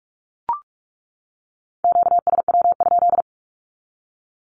Morse code, heard as a keyed tone, about 44 wpm, 700 Hz. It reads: QSW?
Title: Q-Codes 45wpm